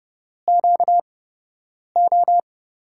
Morse code, heard as a keyed tone, about 31 wpm, spaced as for 8 wpm, 700 Hz. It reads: QO
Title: USA Call Signs - 6 Characters - 8x Character Spacing - 30wpm